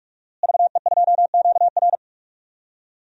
Morse code, heard as a keyed tone, about 41 wpm, 700 Hz. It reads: VE2QR
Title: DXCC-100 Call Signs - 5 Characters - Repeated 3x - 45wpm